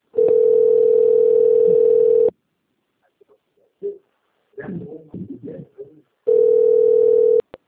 Ringback tone (North America), 440 + 480 Hz.
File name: out- - - - - .48315